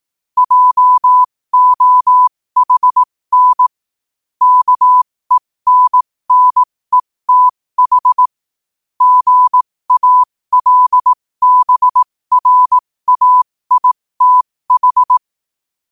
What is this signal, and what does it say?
Morse code, heard as a keyed tone, about 18 wpm, spaced as for 15 wpm, 1000 Hz. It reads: JOHN KENNETH GALBRAITH
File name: qotdfiles / 15wpm